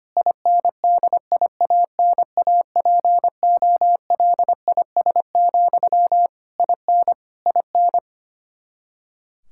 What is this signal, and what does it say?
Morse code, heard as a keyed tone, about 25 wpm, 700 Hz. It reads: INDIANAPOLIS, IN IN
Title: K1USN SST QSO Practice 25wpm